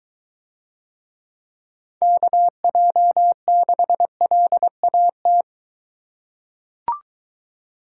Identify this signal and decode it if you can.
Morse code, heard as a keyed tone, about 21 wpm, 700 Hz. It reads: KJ6LAT